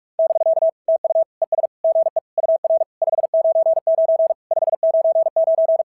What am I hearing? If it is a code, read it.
Morse code, heard as a keyed tone, about 44 wpm, 660 Hz. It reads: <BK> TU ES GE UR 599 599
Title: SOTA QSO Practice 45wpm